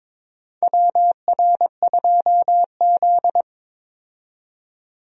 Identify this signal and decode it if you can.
Morse code, heard as a keyed tone, about 21 wpm, 700 Hz. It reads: WR2Z
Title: USA Call Signs - 4 Characters - 22wpm